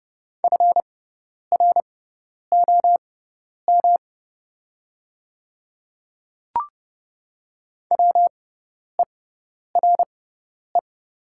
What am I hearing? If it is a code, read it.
Morse code, heard as a keyed tone, about 30 wpm, spaced as for 10 wpm, 700 Hz. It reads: FROM WERE
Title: 6x Character Spacing - Top 100 Words - 30wpm